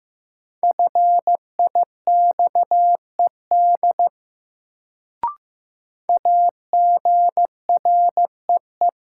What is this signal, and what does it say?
Morse code, heard as a keyed tone, about 15 wpm, 700 Hz. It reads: FIXED AGREE
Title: Sentences from Top 1100 Words - New Word Drill Alt - 15wpm